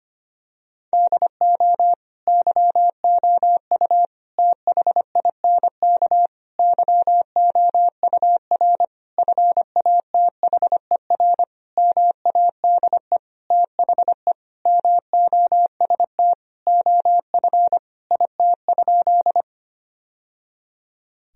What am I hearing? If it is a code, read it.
Morse code, heard as a keyed tone, about 25 wpm, 700 Hz. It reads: DO YOU THINK YOUR FATHER MADE THE MOST OF IT?
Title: Sentences from Top 1000 Words - Review Part 1 - 25wpm